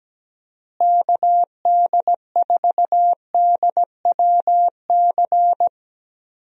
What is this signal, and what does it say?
Morse code, heard as a keyed tone, about 17 wpm, 700 Hz. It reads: KD4DWC
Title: USA Call Signs - 6 Characters - Repeated 3x - 17wpm